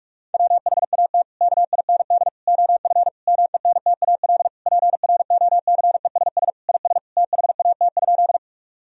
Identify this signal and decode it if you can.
Morse code, heard as a keyed tone, about 41 wpm, 700 Hz. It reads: WHAT KIND OF MENTAL PROCESS IS THAT?
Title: Sentences from Top 1000 Words - Review Part 4 - 45wpm